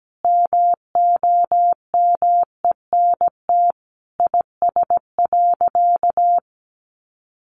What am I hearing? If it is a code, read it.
Morse code, heard as a keyed tone, about 17 wpm, 700 Hz. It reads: MOMENT IS.
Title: Sentences from Top 1000 Words 17wpm